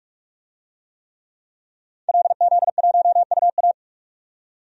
Morse code, heard as a keyed tone, about 42 wpm, 700 Hz. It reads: LZ1UA
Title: DXCC-100 Call Signs - 5 Characters - 45wpm